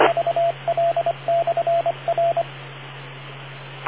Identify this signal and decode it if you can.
Morse code, heard as a keyed tone, about 25 wpm, 680 Hz, with radio noise. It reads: VL/R